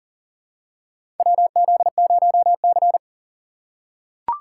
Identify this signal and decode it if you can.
Morse code, heard as a keyed tone, about 40 wpm, 700 Hz. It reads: WZ0C
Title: USA Call Signs - 4 Characters - 2x Character Spacing - 40wpm